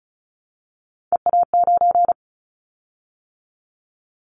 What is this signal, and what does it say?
Morse code, heard as a keyed tone, about 36 wpm, 700 Hz. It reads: EA9
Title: Callsign Prefixes for Top 100 DXCC - 35wpm